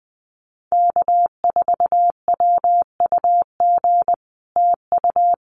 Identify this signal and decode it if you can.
Morse code, heard as a keyed tone, about 20 wpm, 700 Hz. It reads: K4WUG TU
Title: Common QSO Phrases Encore 20wpm